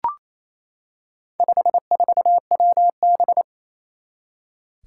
Morse code, heard as a keyed tone, about 27 wpm, 700 Hz. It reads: 54WB
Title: Four Challenging Letter-Number Combinations 28wpm